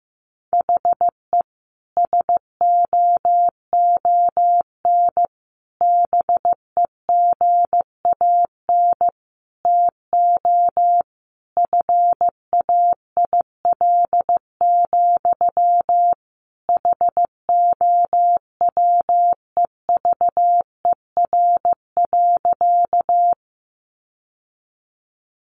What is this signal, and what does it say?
Morse code, heard as a keyed tone, about 15 wpm, 700 Hz. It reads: HE SOON BEGAN TO FAIL, HOWEVER.
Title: Sentences from Top 1100 Words 15wpm